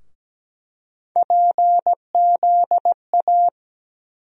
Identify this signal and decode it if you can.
Morse code, heard as a keyed tone, about 17 wpm, 700 Hz. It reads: PZA
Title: USA Call Sign Postfixes - 3 Letters 17wpm